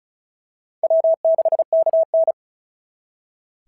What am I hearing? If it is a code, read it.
Morse code, heard as a keyed tone, about 36 wpm, 690 Hz. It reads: W6KN
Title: SOTA QSO Practice 35wpm